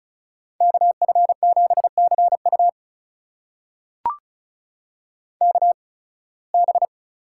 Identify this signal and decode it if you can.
Morse code, heard as a keyed tone, about 34 wpm, spaced as for 23 wpm, 700 Hz. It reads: KF7CU KB
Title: USA Call Signs - 5 Characters - 8x Character Spacing - 35wpm